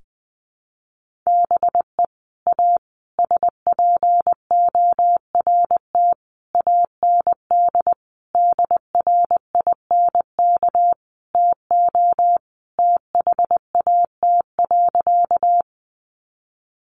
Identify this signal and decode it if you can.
Morse code, heard as a keyed tone, about 20 wpm, 700 Hz. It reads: BE A SPORT AND DRINK TO THAT.
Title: Sentences from Top 1500 Words 20wpm